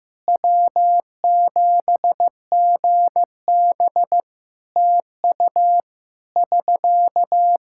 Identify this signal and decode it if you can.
Morse code, heard as a keyed tone, about 16 wpm, 700 Hz. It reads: W7GB TU <SK>
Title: K1USN SST QSO Practice 15wpm